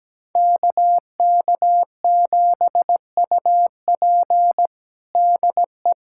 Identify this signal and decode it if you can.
Morse code, heard as a keyed tone, about 17 wpm, 700 Hz. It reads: KK7UP DE
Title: USA Callsign DE Callsign - 5 Characters - 17wpm